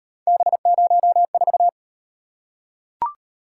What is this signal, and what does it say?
Morse code, heard as a keyed tone, about 37 wpm, 700 Hz. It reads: B04